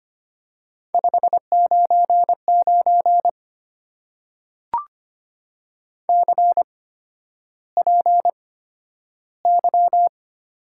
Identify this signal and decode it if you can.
Morse code, heard as a keyed tone, about 25 wpm, spaced as for 11 wpm, 700 Hz. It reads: 599 CPY
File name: Top 90 QSO Elements - 8x Character Spacing - 25wpm